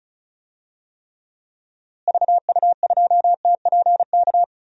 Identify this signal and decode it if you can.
Morse code, heard as a keyed tone, about 34 wpm, 700 Hz. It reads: VU2TPK
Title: DXCC-100 Call Signs - 6 Characters - Repeated 3x - 35wpm